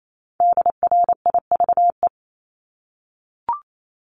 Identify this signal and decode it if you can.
Morse code, heard as a keyed tone, about 28 wpm, 700 Hz. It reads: DRIVE
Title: Top 500 Words Repeated 28wpm